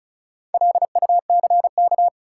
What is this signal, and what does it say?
Morse code, heard as a keyed tone, about 36 wpm, 700 Hz. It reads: LUCK
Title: Sentences from Top 1500 Words - New Word Drill - 35wpm